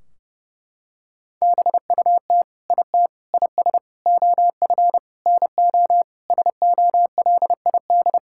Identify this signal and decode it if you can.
Morse code, heard as a keyed tone, about 30 wpm, 700 Hz. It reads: BUT IT IS OF NO SOLID
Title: Sentences from Top 1600 Words 30wpm